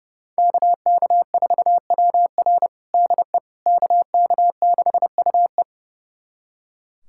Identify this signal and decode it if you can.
Morse code, heard as a keyed tone, about 30 wpm, 700 Hz. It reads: KK4WR DE KK6UE
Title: USA Callsign DE Callsign - 5 Characters - 30wpm